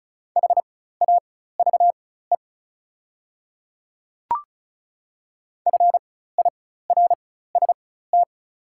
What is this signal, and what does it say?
Morse code, heard as a keyed tone, about 36 wpm, spaced as for 16 wpm, 700 Hz. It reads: HAVE FIRST
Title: 4x Character Spacing - Top 200 Words - 35wpm